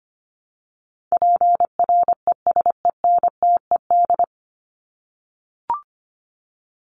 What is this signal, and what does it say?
Morse code, heard as a keyed tone, about 25 wpm, 700 Hz. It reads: PRESENTED
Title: ED Suffix - Top 100 Words with Base Words 25wpm